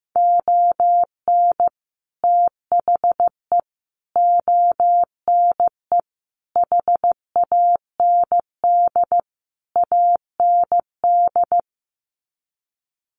Morse code, heard as a keyed tone, about 15 wpm, 700 Hz. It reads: ON THE ONE HAND AND
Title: Sets of 5 Words Encore2 15wpm